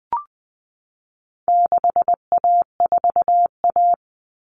Morse code, heard as a keyed tone, about 20 wpm, 700 Hz. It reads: 6A4A
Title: Common Cut Numbers - 4-Digit 20wpm